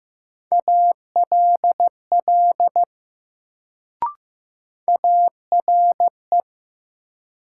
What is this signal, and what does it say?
Morse code, heard as a keyed tone, about 15 wpm, 700 Hz. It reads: ALL ARE
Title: Sentences from Top 100 Words - New Word Drill - 15wpm